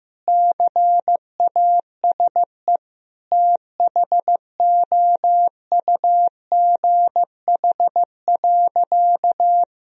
Morse code, heard as a keyed tone, about 16 wpm, 700 Hz. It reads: CASE THOUGH.
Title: Sentences from Top 1000 Words - Review Part 2 - 15wpm